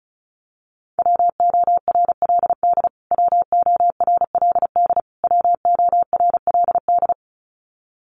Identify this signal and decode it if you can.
Morse code, heard as a keyed tone, about 34 wpm, 700 Hz. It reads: WORLD WORLD WORLD
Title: Top 500 Words Repeated 35wpm